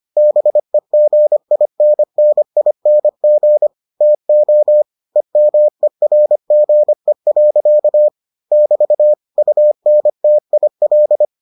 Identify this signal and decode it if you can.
Morse code, heard as a keyed tone, about 25 wpm, 600 Hz. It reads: BEGINNING TO EMERGE. <BT> UNTIL